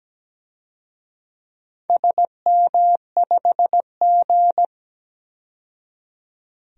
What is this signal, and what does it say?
Morse code, heard as a keyed tone, about 17 wpm, 700 Hz. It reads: SM5G
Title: DXCC-100 Call Signs - 4 Characters - 17wpm